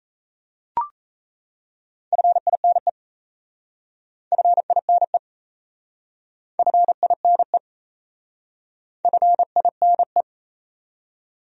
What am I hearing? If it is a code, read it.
Morse code, heard as a keyed tone, about 38 wpm, 700 Hz. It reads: FINE FINE FINE FINE